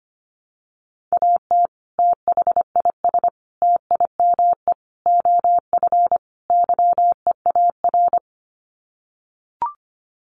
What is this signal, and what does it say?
Morse code, heard as a keyed tone, about 25 wpm, 700 Hz. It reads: AT THIS TIME OF YEAR
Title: Sets of 5 Words Encore2 25wpm